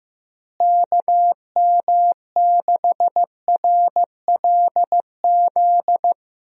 Morse code, heard as a keyed tone, about 15 wpm, 700 Hz. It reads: KM6RLZ
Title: USA Call Signs - 6 Characters - Repeated 3x - 15wpm